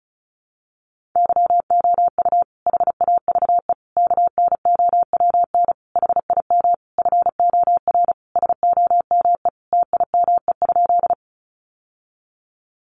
Morse code, heard as a keyed tone, about 34 wpm, 700 Hz. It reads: YOU HAVE KNOWN HIM FOR SOME TIME?
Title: Sentences from Top 600 Words 35wpm